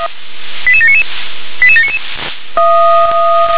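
Five-tone selective call (ZVEI), heard as quick runs of five tones, tones of about 70 ms. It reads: 88799 88799
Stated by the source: Five-tone ZVEI selective call received with an SDR.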